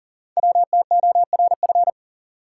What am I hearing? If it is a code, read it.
Morse code, heard as a keyed tone, about 39 wpm, 700 Hz. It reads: WTORF
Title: Code Groups - 5 Letter:Numbers - 40wpm